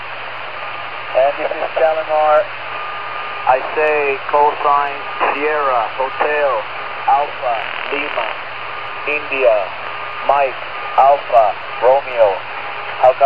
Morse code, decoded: none

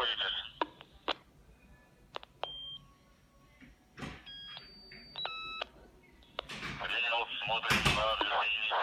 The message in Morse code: none

